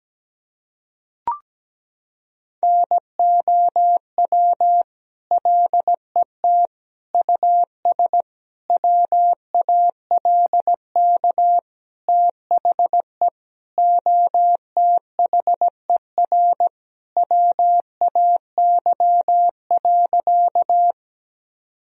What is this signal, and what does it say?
Morse code, heard as a keyed tone, about 17 wpm, 700 Hz. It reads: NOW LET US WALK THE OTHER WAY.
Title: Sentences from Top 1000 Words - Review Part 1 - 17wpm